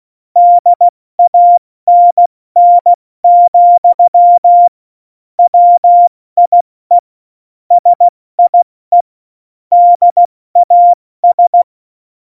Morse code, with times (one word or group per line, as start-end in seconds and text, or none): 0.4-4.7 DANN,
5.4-7.0 WIE
7.7-9.0 SIE
9.7-11.6 DAS